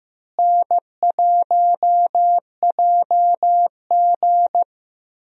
Morse code, heard as a keyed tone, about 14 wpm, 700 Hz. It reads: N1JG
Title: USA Call Signs - 4 Characters - Repeated 3x - 15wpm